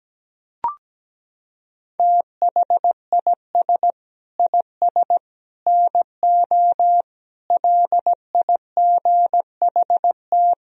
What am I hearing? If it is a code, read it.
Morse code, heard as a keyed tone, about 17 wpm, 700 Hz. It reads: THIS IS NO LIGHT
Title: Sentences from Top 1000 Words - Review Part 1 - 17wpm